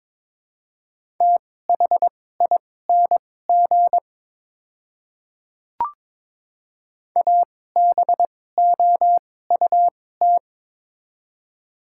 Morse code, heard as a keyed tone, about 23 wpm, spaced as for 16 wpm, 700 Hz. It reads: THING ABOUT
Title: 2x Character Spacing - Top 100 Words - 22wpm